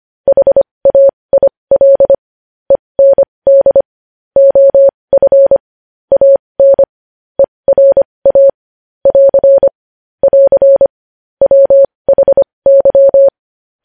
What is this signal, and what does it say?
Morse code, heard as a keyed tone, about 25 wpm, spaced as for 20 wpm, 550 Hz. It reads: HAIL END OF AN ERA <AR> <AR> WHY
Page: News Headlines in Morse Code at 20 WPM